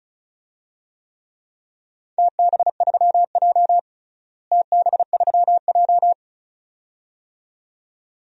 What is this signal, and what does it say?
Morse code, heard as a keyed tone, about 34 wpm, 700 Hz. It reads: TB3J TB3J
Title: DXCC-100 Call Signs - 4 Characters - Repeated 2x - 35wpm